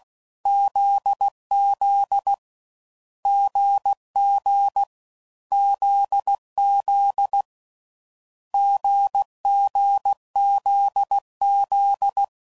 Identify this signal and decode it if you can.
Morse code, heard as a keyed tone, about 16 wpm, 800 Hz. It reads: ZZ GG ZZ GGZZ
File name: Combinations / Difficult Combos / GZ GZ7